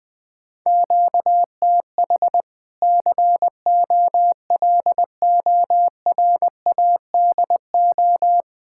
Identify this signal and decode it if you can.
Morse code, heard as a keyed tone, about 20 wpm, 700 Hz. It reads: QTH COLORADO
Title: Practice QSO - Exchange 1 - Beginning - 20wpm